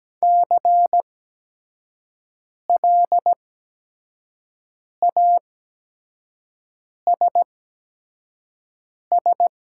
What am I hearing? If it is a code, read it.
Morse code, heard as a keyed tone, about 17 wpm, spaced as for 5 wpm, 700 Hz. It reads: CLASS
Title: 8x Character Spacing - Top 500 Words - 17wpm